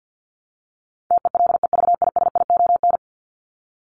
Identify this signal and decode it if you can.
Morse code, heard as a keyed tone, about 50 wpm, 700 Hz. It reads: TELEVISION
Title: Top 501-750 Words 50wpm